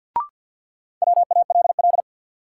Morse code, heard as a keyed tone, about 52 wpm, 700 Hz. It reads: WALL
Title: Sentences from Top 1000 Words - Review Part 6 - New Word Drill - 50wpm